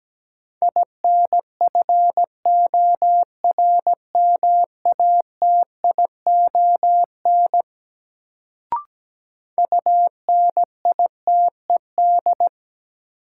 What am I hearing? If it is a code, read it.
Morse code, heard as a keyed tone, about 17 wpm, 700 Hz. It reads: INFORMATION UNITED
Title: Sentences from Top 600 Words - New Word Drill - 17wpm